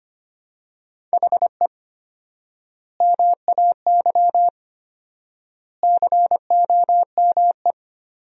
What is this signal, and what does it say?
Morse code, heard as a keyed tone, about 25 wpm, 700 Hz. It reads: HE MAY COME
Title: 4x Word-Spacing - Sentences from Top 300 Words - 25wpm